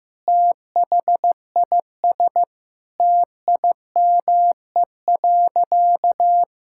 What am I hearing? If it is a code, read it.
Morse code, heard as a keyed tone, about 15 wpm, 700 Hz. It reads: THIS TIME.
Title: Sentences from Top 1000 Words - Review Part 5 - 15wpm